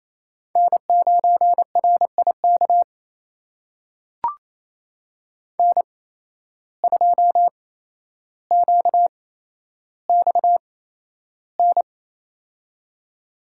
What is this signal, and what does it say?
Morse code, heard as a keyed tone, about 28 wpm, spaced as for 12 wpm, 700 Hz. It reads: N9RIK N2QXN
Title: USA Call Signs - 5 Characters - 8x Character Spacing - 28wpm